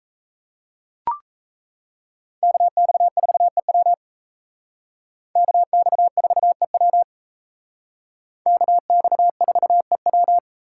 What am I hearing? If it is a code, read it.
Morse code, heard as a keyed tone, about 36 wpm, 700 Hz. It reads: KX4EW KX4EW KX4EW